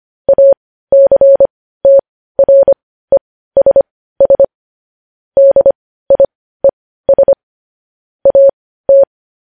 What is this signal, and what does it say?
Morse code, heard as a keyed tone, about 25 wpm, spaced as for 15 wpm, 550 Hz. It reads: ACTRESS DIES AT
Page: news headlines in morse code at 15 wpm